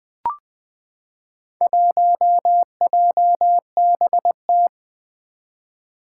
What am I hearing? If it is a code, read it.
Morse code, heard as a keyed tone, about 20 wpm, 700 Hz. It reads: 1JBT
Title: Four Challenging Letter-Number Combinations 20wpm